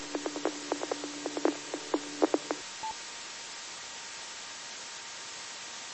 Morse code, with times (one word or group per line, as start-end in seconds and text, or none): none